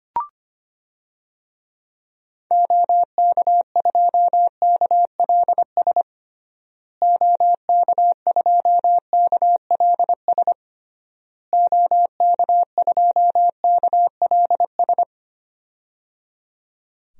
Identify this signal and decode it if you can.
Morse code, heard as a keyed tone, about 25 wpm, 700 Hz. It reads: OK2KLS OK2KLS OK2KLS
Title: DXCC-100 Call Signs - 6 Characters - Repeated 3x - 25wpm